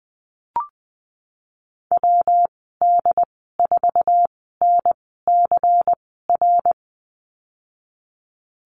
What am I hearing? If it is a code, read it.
Morse code, heard as a keyed tone, about 20 wpm, spaced as for 15 wpm, 700 Hz. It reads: WD4NCR
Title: 173 Lesson 28g - Number 9 - Call Signs 20wpm